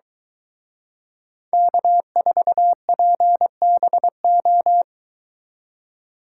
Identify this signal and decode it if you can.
Morse code, heard as a keyed tone, about 23 wpm, 700 Hz. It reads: K4PBO